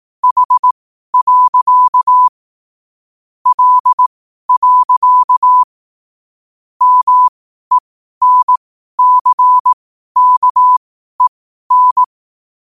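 Morse code, heard as a keyed tone, about 18 wpm, spaced as for 12 wpm, 1000 Hz. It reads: H. L. MENCKEN
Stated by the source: Morse code sent at 12 words per minute.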